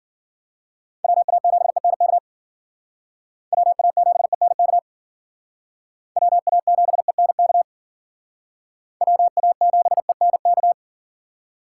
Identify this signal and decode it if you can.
Morse code, heard as a keyed tone, about 54 wpm, 700 Hz. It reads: WA7ENK WA7ENK WA7ENK WA7ENK